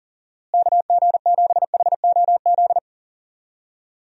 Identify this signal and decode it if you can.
Morse code, heard as a keyed tone, about 39 wpm, 700 Hz. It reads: KG7HOZ
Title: USA Call Signs - 6 Characters - Repeated 3x - 40wpm